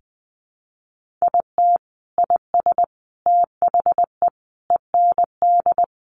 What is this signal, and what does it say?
Morse code, heard as a keyed tone, about 20 wpm, 700 Hz. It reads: IT IS THE END
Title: Sentences from Top 1500 Words 20wpm